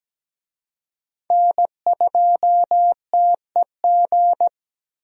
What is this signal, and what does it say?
Morse code, heard as a keyed tone, about 17 wpm, 700 Hz. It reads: N2TEG
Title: USA Call Signs - 5 Characters - Repeated 2x - 17wpm